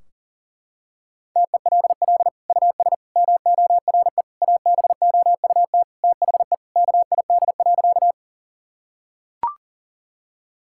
Morse code, heard as a keyed tone, about 40 wpm, 700 Hz. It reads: TELL US MORE ABOUT THE KID.
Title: Sentences from Top 1600 Words 40wpm